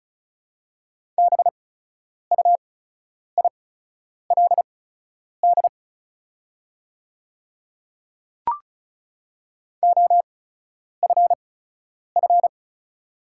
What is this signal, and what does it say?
Morse code, heard as a keyed tone, about 36 wpm, spaced as for 10 wpm, 700 Hz. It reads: BUILD OFF
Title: 8x Character Spacing - Top 200 Words - 35wpm